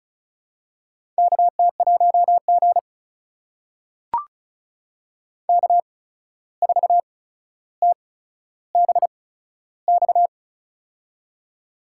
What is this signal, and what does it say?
Morse code, heard as a keyed tone, about 36 wpm, spaced as for 14 wpm, 700 Hz. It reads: KT1G K4TBX
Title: USA Call Signs - 8x Character Spacing - 35wpm